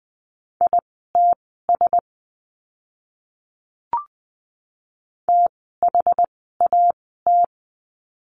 Morse code, heard as a keyed tone, about 20 wpm, spaced as for 14 wpm, 700 Hz. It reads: ITS THAT
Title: 237 Lesson 38f - Letter q - Words 20wpm